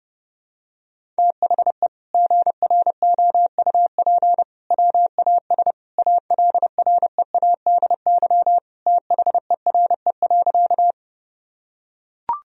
Morse code, heard as a keyed tone, about 30 wpm, 700 Hz. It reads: THE GROUP WAS ALREADY THERE.
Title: Sentences from Top 1000 Words - Review Part 1 - 30wpm